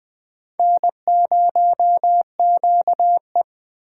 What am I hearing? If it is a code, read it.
Morse code, heard as a keyed tone, about 20 wpm, 700 Hz. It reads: N0QE